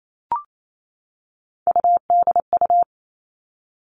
Morse code, heard as a keyed tone, about 28 wpm, 700 Hz. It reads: UDU
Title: Three Challenging Letter-Number Combinations 28wpm